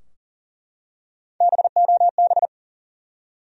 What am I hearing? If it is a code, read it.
Morse code, heard as a keyed tone, about 40 wpm, 700 Hz. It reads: BOB
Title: USA Common Names - Max 4 Letters - 40wpm